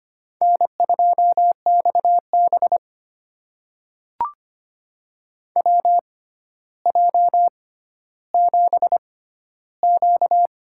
Morse code, heard as a keyed tone, about 25 wpm, spaced as for 13 wpm, 700 Hz. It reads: N2XB WJ7Q
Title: USA Call Signs - 4 Characters - 6x Character Spacing - 25wpm